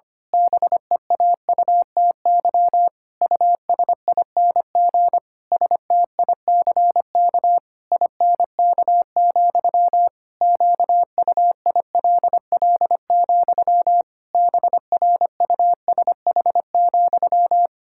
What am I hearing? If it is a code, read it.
Morse code, heard as a keyed tone, about 25 wpm, 700 Hz. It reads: BEAUTY USING STICK INK, QUILL, BRUSH,
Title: Pangram Challenge v2 - 25wpm